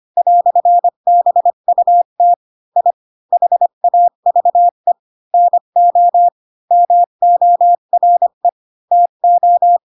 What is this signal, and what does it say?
Morse code, heard as a keyed tone, about 25 wpm, 700 Hz. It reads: "BUT I HAVE NO MORE TO
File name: Pinocchio / Pinocchio ch7 20 25 ik1hge